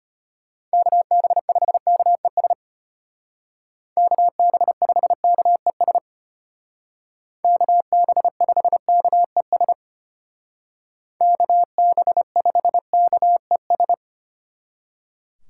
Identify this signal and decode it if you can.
Morse code, heard as a keyed tone, about 33 wpm, 700 Hz. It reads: KB5KES KB5KES KB5KES KB5KES